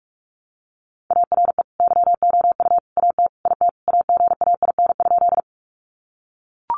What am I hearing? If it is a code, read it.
Morse code, heard as a keyed tone, about 44 wpm, 700 Hz. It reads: ARE YOU AT IT AGAIN?
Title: Sentences from Top 200 Words 45wpm